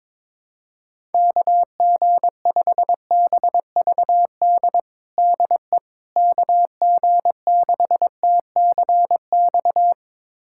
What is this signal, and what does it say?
Morse code, heard as a keyed tone, about 22 wpm, 700 Hz. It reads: KG5BVD DE KG6TCX
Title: USA Callsign DE Callsign 22wpm